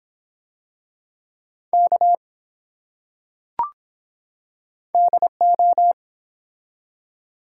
Morse code, heard as a keyed tone, about 24 wpm, 700 Hz. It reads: K DO